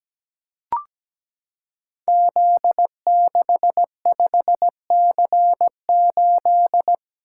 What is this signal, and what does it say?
Morse code, heard as a keyed tone, about 17 wpm, 700 Hz. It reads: Z65C8
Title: Code Groups - 5 Letter:Numbers - 17wpm